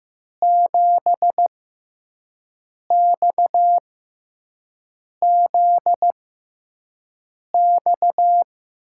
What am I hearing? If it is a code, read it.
Morse code, heard as a keyed tone, about 15 wpm, spaced as for 5 wpm, 700 Hz. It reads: 7XZX
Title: USA Call Signs - 6x Character Spacing - 15wpm